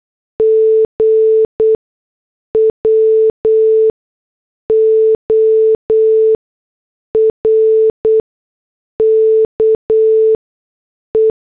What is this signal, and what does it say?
Morse code, heard as a keyed tone, about 8 wpm, spaced as for 6 wpm, 440 Hz. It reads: G WORKE
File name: morse / headlines